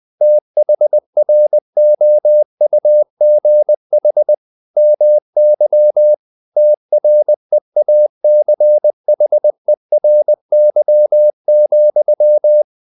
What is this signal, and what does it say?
Morse code, heard as a keyed tone, about 20 wpm, 600 Hz. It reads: THROUGH MY TREACHERY,
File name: the old man and the sea / the old man and the sea part 1 20wpm